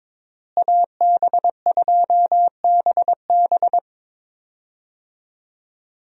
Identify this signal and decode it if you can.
Morse code, heard as a keyed tone, about 21 wpm, 700 Hz. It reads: AB2BB
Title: USA Call Signs - 5 Characters - Repeated 3x - 22wpm